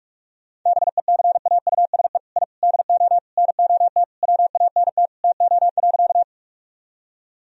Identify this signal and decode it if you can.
Morse code, heard as a keyed tone, about 46 wpm, 700 Hz. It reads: BECAUSE I DO NOT WANT TO.
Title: Sentences from Top 1000 Words - Review Part 3 - 45wpm